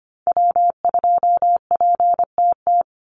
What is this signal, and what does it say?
Morse code, heard as a keyed tone, about 25 wpm, 700 Hz. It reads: W2PTT
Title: Call Signs - 25wpm